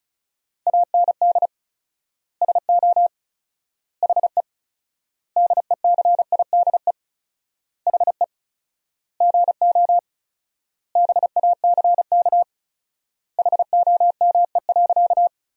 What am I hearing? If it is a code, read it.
Morse code, heard as a keyed tone, about 36 wpm, 700 Hz. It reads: AND SO HE DECIDE HE GO BACK HOME.
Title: 4x Word-Spacing - Sentences from Top 500 Words - 35wpm